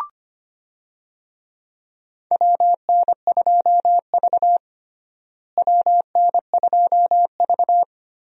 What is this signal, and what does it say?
Morse code, heard as a keyed tone, about 25 wpm, 700 Hz. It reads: WN2V WN2V
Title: USA Call Signs - 4 Characters - Repeated 3x - 25wpm